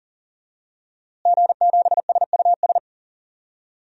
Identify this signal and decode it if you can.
Morse code, heard as a keyed tone, about 40 wpm, 700 Hz. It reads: G7SUS
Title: DXCC-100 Call Signs - 5 Characters - Repeated 3x - 40wpm